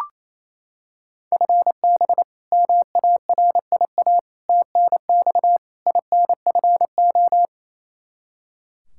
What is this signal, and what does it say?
Morse code, heard as a keyed tone, about 28 wpm, 700 Hz. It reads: FB MARIA TNX INFO
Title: Practice QSO - Exchange 3 - More Info - 28wpm